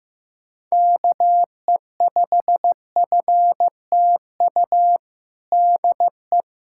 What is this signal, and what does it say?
Morse code, heard as a keyed tone, about 16 wpm, 700 Hz. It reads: KE5FTU DE
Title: USA Callsign DE Callsign 15wpm